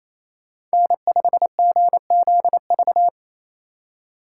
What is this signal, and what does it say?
Morse code, heard as a keyed tone, about 37 wpm, 700 Hz. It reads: N5GZV